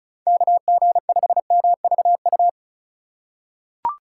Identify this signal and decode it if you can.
Morse code, heard as a keyed tone, about 36 wpm, 700 Hz. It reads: KG5MVU
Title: USA Call Signs - 6 Characters - 2x Character Spacing - 35wpm